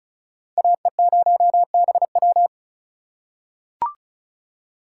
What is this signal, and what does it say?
Morse code, heard as a keyed tone, about 36 wpm, 700 Hz. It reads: AE0BW
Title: USA Call Signs - 5 Characters - 4x Character Spacing - 35wpm